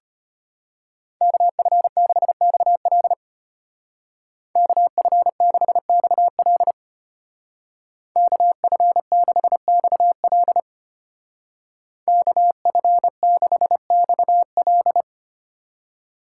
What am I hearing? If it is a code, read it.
Morse code, heard as a keyed tone, about 32 wpm, 700 Hz. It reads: KF6XL KF6XL KF6XL KF6XL